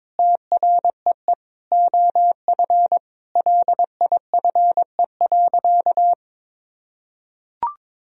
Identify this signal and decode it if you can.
Morse code, heard as a keyed tone, about 22 wpm, 700 Hz. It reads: TREE OF LIFE.
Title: Sentences from Top 1000 Words - Review Part 1 - 22wpm